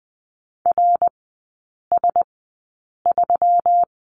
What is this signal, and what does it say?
Morse code, heard as a keyed tone, about 20 wpm, spaced as for 8 wpm, 700 Hz. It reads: RS3
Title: 121 Lesson 20e - Number 3 - Three Characters 20wpm